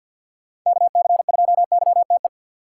Morse code, heard as a keyed tone, about 51 wpm, 690 Hz. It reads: KC2YTE
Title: USA Call Signs - 3x Character Spacing - 50wpm